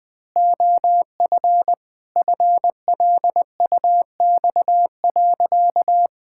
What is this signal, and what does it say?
Morse code, heard as a keyed tone, about 21 wpm, 700 Hz. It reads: OF FLUX.
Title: Pangram Challenge v2 - 20wpm